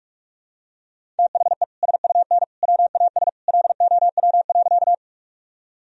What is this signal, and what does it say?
Morse code, heard as a keyed tone, about 46 wpm, 700 Hz. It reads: THE SUN WAS LOW.
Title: Sentences from Top 1000 Words - Review Part 1 - 45wpm